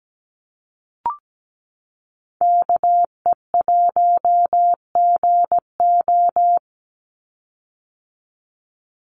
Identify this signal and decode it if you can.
Morse code, heard as a keyed tone, about 17 wpm, 700 Hz. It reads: KE1GO
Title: USA Call Signs 17wpm